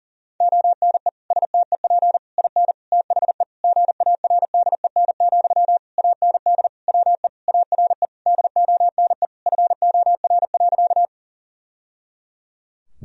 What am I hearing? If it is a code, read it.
Morse code, heard as a keyed tone, about 40 wpm, 700 Hz. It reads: ONE STEP IN THE GARDEN, AND WE ARE DONE FOR.
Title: Sentences from Top 1000 Words - Review Part 4 - 40wpm